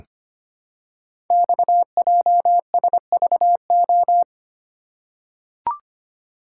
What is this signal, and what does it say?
Morse code, heard as a keyed tone, about 25 wpm, 700 Hz. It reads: XJSVO
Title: Code Groups - 5 Letters - 25wpm